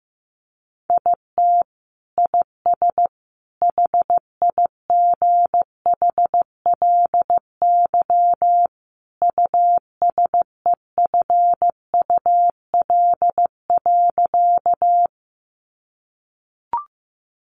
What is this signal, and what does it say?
Morse code, heard as a keyed tone, about 15 wpm, 700 Hz. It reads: IT IS HIGHLY USEFUL.
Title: Sentences from Top 1100 Words 15wpm